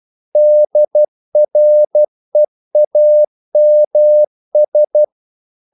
Morse code, heard as a keyed tone, about 12 wpm, 600 Hz. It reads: DREAMS
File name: CW / Practice / If12